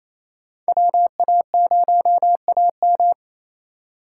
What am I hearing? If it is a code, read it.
Morse code, heard as a keyed tone, about 28 wpm, 700 Hz. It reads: WA0AM
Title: USA Call Signs - 5 Characters - 2x Character Spacing - 28wpm